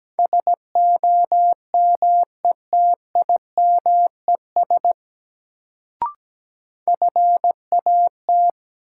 Morse code, heard as a keyed tone, about 17 wpm, 700 Hz. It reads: SOMETIMES FAT
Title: Sentences from Top 600 Words - New Word Drill - 17wpm